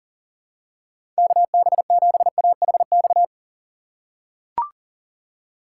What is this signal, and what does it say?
Morse code, heard as a keyed tone, about 37 wpm, 700 Hz. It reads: KB7AHX
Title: USA Call Signs - 6 Characters - 6x Character Spacing - 40wpm